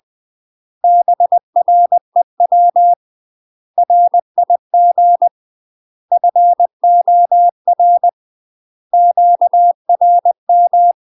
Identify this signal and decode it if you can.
Morse code, heard as a keyed tone, about 20 wpm, 700 Hz. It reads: BREW RIG FOR QRM
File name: UK MASTER QSOs TRAINING / 20WPM / UK MASTER QSOs 0022